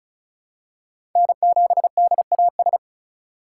Morse code, heard as a keyed tone, about 36 wpm, 700 Hz. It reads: N7DAS
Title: USA Call Signs - 5 Characters - 2x Character Spacing - 35wpm